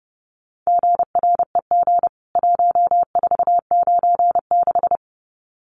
Morse code, heard as a keyed tone, about 30 wpm, 700 Hz. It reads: GREG 1496
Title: CWT Exchange 30wpm